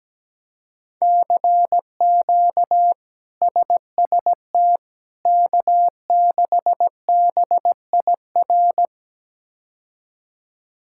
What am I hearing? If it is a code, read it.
Morse code, heard as a keyed tone, about 17 wpm, 700 Hz. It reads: CQ SST K6BIR
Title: K1USN SST QSO Practice - 17wpm